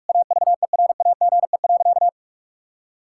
Morse code, heard as a keyed tone, about 41 wpm, 700 Hz. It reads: AVERAGE.
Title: Sentences from Top 1000 Words - Review Part 6 - 45wpm